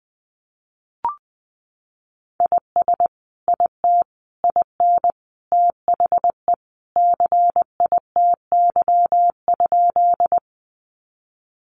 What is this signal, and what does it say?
Morse code, heard as a keyed tone, about 20 wpm, 700 Hz. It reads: IS IT IN THE CITY?
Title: Sentences from Top 300 Words 20wpm